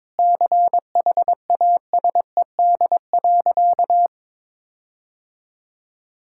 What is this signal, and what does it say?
Morse code, heard as a keyed tone, about 22 wpm, 700 Hz. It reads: CHASED.
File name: practice / 22wpm / Sentences with 4-Word Phrases - 22wpm